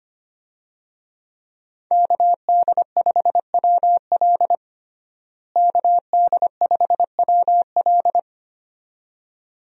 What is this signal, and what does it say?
Morse code, heard as a keyed tone, about 25 wpm, 700 Hz. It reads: KD5WL KD5WL
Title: USA Call Signs - 5 Characters - Repeated 2x - 25wpm